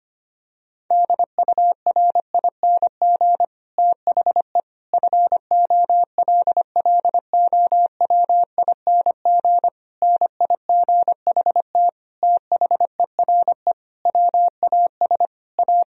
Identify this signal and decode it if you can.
Morse code, heard as a keyed tone, about 25 wpm, 700 Hz. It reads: DURING THE FOLLOWING NIGHT THERE WAS A
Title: Sentences from Top 1000 Words - Review Part 4 - 25wpm